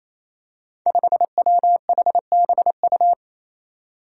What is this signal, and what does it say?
Morse code, heard as a keyed tone, about 28 wpm, 700 Hz. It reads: 5WHBU
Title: Five Challenging Letter-Number Combinations 28wpm